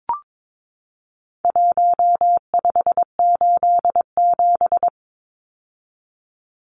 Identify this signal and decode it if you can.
Morse code, heard as a keyed tone, about 22 wpm, 700 Hz. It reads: 1587
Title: Numbers - Contest Style - 4-Digit 22wpm